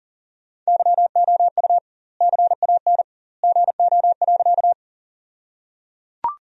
Morse code, heard as a keyed tone, about 37 wpm, 700 Hz. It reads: YOU CAN GO.
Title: 2x Word-Spacing - Sentences from Top 100 Words - 40wpm